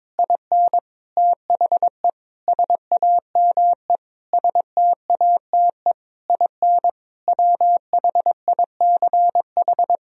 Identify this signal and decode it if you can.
Morse code, heard as a keyed tone, about 22 wpm, 700 Hz. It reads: IN THE SAME STATE IN WHICH
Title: Sentences from Top 1000 Words - Review Part 2 - 22wpm